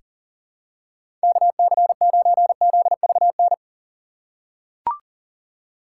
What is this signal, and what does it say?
Morse code, heard as a keyed tone, about 40 wpm, 700 Hz. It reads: KC9ZVN
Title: USA Call Signs - 6 Characters - 3x Character Spacing - 40wpm